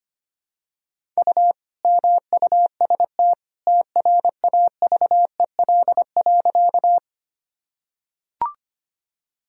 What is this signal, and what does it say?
Morse code, heard as a keyed tone, about 25 wpm, 700 Hz. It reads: U MUST TRAVEL.
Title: Sentences from Top 400 Words and Abbreviations 25wpm